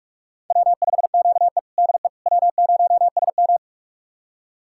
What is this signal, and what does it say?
Morse code, heard as a keyed tone, about 46 wpm, 700 Hz. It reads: W5QE DE W0SM
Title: USA Callsign DE Callsign - 4 Characters - 45wpm